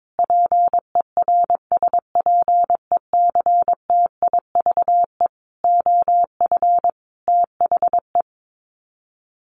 Morse code, heard as a keyed tone, about 22 wpm, 700 Hz. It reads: PERSPECTIVE OF THE
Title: Sets of 5 Words Encore2 22wpm